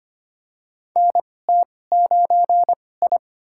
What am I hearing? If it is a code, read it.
Morse code, heard as a keyed tone, about 25 wpm, spaced as for 18 wpm, 700 Hz. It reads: NT9I
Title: USA Call Signs - 2x Character Spacing - 25wpm